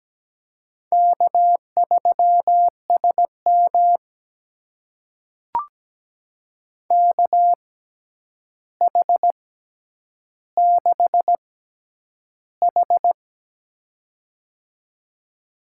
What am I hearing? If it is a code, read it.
Morse code, heard as a keyed tone, about 17 wpm, spaced as for 9 wpm, 700 Hz. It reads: K3SM KH6H
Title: USA Call Signs - 4 Characters - 6x Character Spacing - 17wpm